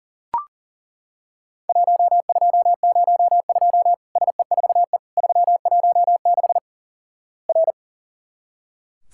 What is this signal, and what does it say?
Morse code, heard as a keyed tone, about 37 wpm, 700 Hz. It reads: 1202 SE4E 316 R
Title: Worked All Europe DX Contest - QTC Exchange - 40wpm